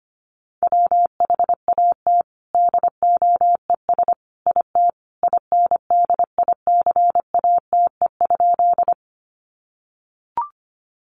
Morse code, heard as a keyed tone, about 25 wpm, 700 Hz. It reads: WHAT DOES IT INDICATE?
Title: Sentences from Top 1300 Words 25wpm